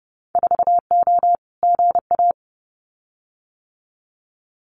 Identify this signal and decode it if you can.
Morse code, heard as a keyed tone, about 30 wpm, 700 Hz. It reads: 4O GA
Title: Winter Field Day Exchanges 30wpm